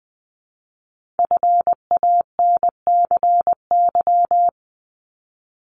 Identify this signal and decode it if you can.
Morse code, heard as a keyed tone, about 20 wpm, 700 Hz. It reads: FANCY
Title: Multi-letter Phonemes Repeated 3x 20wpm